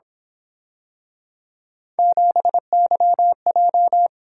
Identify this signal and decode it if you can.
Morse code, heard as a keyed tone, about 26 wpm, 700 Hz. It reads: 7YJ
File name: Speed-Racing - Three Letter-Number Combinations - 22wpm